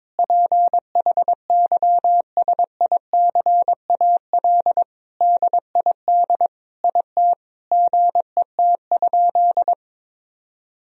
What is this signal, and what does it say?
Morse code, heard as a keyed tone, about 22 wpm, 700 Hz. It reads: PHYSICAL DID IT GET?
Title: Sentences from Top 1000 Words - Review Part 4 - 22wpm